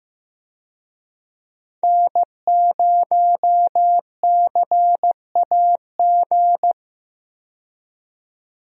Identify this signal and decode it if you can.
Morse code, heard as a keyed tone, about 15 wpm, 700 Hz. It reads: N0CAG
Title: USA Call Signs - 5 Characters - 15wpm